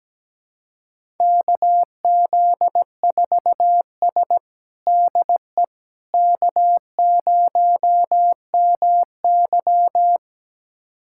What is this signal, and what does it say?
Morse code, heard as a keyed tone, about 17 wpm, 700 Hz. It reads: KZ4S DE K0MY
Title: USA Callsign DE Callsign - 4 Characters - 17wpm